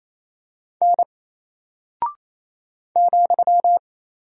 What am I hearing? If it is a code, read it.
Morse code, heard as a keyed tone, about 27 wpm, 700 Hz. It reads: N ,
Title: Single Character (Letter:Number:Punctuation:Prosign) Alt 28wpm